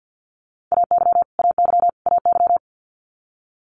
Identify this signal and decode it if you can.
Morse code, heard as a keyed tone, about 50 wpm, 690 Hz. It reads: AY AY AY
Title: Multi-letter Phonemes Repeated 3x 50wpm